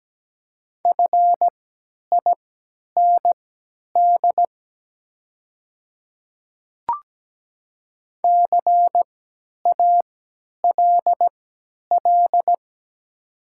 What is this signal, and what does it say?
Morse code, heard as a keyed tone, about 17 wpm, spaced as for 10 wpm, 700 Hz. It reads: FIND CALL
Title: 3x Character Spacing - Top 100 Words - 17wpm